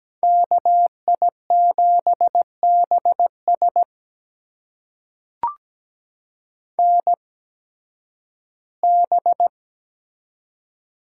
Morse code, heard as a keyed tone, about 17 wpm, spaced as for 9 wpm, 700 Hz. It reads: KI7BSN B
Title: USA Call Signs - 8x Character Spacing - 17wpm